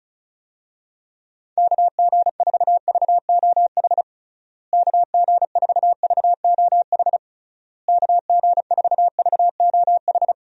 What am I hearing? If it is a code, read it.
Morse code, heard as a keyed tone, about 34 wpm, 700 Hz. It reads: KG4VOH KG4VOH KG4VOH
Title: USA Call Signs - 6 Characters - Repeated 3x - 35wpm